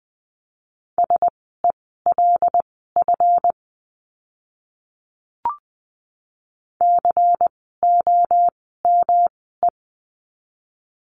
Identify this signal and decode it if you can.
Morse code, heard as a keyed tone, about 20 wpm, spaced as for 14 wpm, 700 Hz. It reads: SELF COME
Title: 139 Lesson 23f - Letter f - Words 20wpm